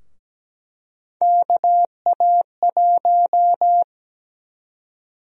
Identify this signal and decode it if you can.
Morse code, heard as a keyed tone, about 17 wpm, 700 Hz. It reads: KA1
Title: USA Call Sign Prefixes - Repeated 2x - 17wpm